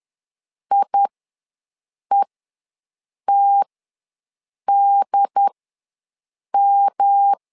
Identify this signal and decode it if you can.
Morse code, heard as a keyed tone, about 10 wpm, spaced as for 6 wpm, 800 Hz. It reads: IETDM
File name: LEZIONI / 3 MWRO / 3 MWR ws 10wpm 2151